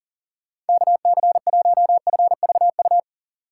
Morse code, heard as a keyed tone, about 53 wpm, 700 Hz. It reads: KC1FVU